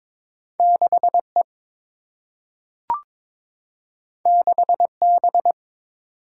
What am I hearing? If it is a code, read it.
Morse code, heard as a keyed tone, about 22 wpm, 700 Hz. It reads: 6 E 6 B
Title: Single-Letter Single-Number 22wpm